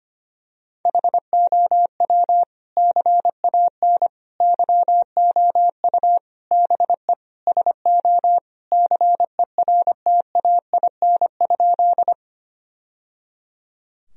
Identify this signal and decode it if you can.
Morse code, heard as a keyed tone, about 25 wpm, 700 Hz. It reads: HOW CAN YOU BE SO CERTAIN?
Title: Sentences from Top 1000 Words - Review Part 2 - 25wpm